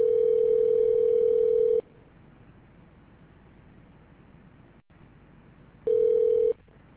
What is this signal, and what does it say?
Ringback tone (North America), 440 + 480 Hz.